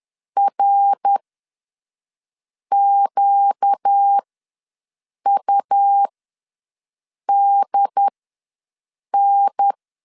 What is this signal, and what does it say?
Morse code, heard as a keyed tone, about 10 wpm, spaced as for 5 wpm, 800 Hz. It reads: RQUDN
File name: LEZIONI / 5 LBVC / 5 L ws 10wpm 3636